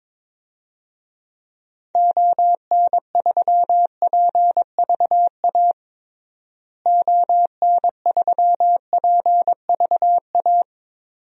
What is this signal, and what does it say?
Morse code, heard as a keyed tone, about 21 wpm, 700 Hz. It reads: ON3PVA ON3PVA
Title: DXCC-100 Call Signs - 6 Characters - Repeated 2x - 22wpm